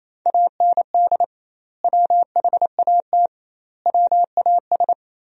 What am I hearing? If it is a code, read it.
Morse code, heard as a keyed tone, about 28 wpm, 700 Hz. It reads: AND WHAT WAS
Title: 2x Word-Spacing - Sentences from Top 200 Words - 28wpm